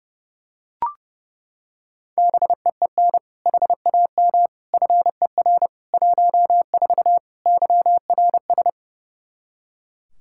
Morse code, heard as a keyed tone, about 32 wpm, 700 Hz. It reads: BEEN HAM FER 14 YRS
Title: Practice QSO - Exchange 3 - More Info - 30wpm